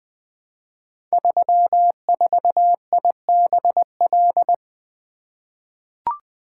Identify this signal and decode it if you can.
Morse code, heard as a keyed tone, about 20 wpm, 700 Hz. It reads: 34IBL
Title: Code Groups - 5 Letter:Numbers - 20wpm